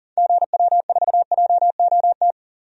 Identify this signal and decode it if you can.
Morse code, heard as a keyed tone, about 40 wpm, 700 Hz. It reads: GW4JOT
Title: DXCC-100 Call Signs - 6 Characters - 40wpm